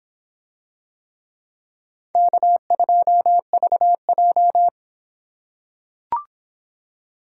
Morse code, heard as a keyed tone, about 24 wpm, 700 Hz. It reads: K2VJ